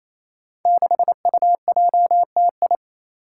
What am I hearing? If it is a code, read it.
Morse code, heard as a keyed tone, about 28 wpm, 700 Hz. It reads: 6UJTI
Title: Five Challenging Letter-Number Combinations 28wpm